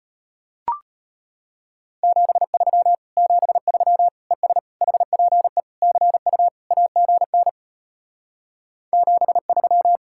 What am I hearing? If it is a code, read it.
Morse code, heard as a keyed tone, about 33 wpm, 700 Hz. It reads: 73 73 ES HPE CU AGN 73